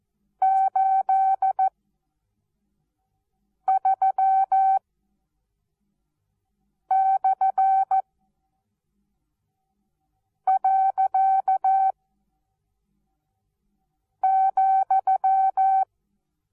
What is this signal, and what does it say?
Morse code, heard as a keyed tone, about 14 wpm, spaced as for 4 wpm, 770 Hz. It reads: 83/.,
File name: Onlinelehrgang / morsen / DJ4UF Morsen19